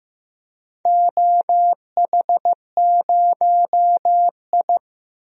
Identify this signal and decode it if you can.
Morse code, heard as a keyed tone, about 15 wpm, 700 Hz. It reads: OH0I
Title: DXCC-100 Call Signs - 4 Characters - 15wpm